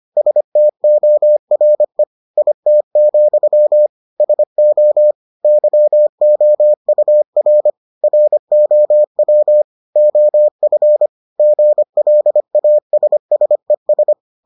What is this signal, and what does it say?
Morse code, heard as a keyed tone, about 25 wpm, 600 Hz. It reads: STORE IT, SO YOUR ROW OF GLASSES